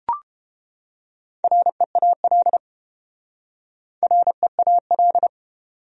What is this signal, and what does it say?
Morse code, heard as a keyed tone, about 29 wpm, 700 Hz. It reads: REAL REAL